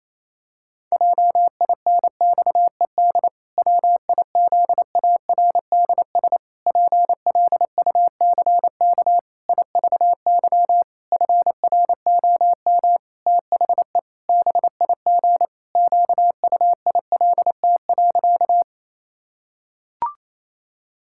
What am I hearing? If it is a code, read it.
Morse code, heard as a keyed tone, about 28 wpm, 700 Hz. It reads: JINXED WIZARDS PLUCK IVY FROM THE BIG QUILT.
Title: Pangram Challenge v2 - 28wpm